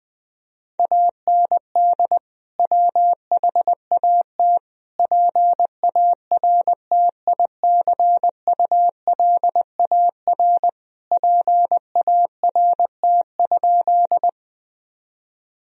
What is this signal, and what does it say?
Morse code, heard as a keyed tone, about 20 wpm, 700 Hz. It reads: AND WHAT PARTICULAR PART?
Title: Sentences from Top 1000 Words - Review Part 4 - 20wpm